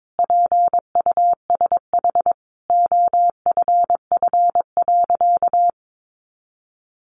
Morse code, heard as a keyed tone, about 22 wpm, 700 Hz. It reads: PUSH OFF.
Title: Sentences from Top 1300 Words 22wpm